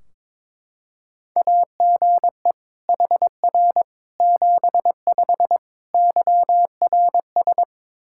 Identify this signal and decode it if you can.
Morse code, heard as a keyed tone, about 23 wpm, 700 Hz. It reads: AGE HR 75 YRS
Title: K1USN SST QSO Practice 22wpm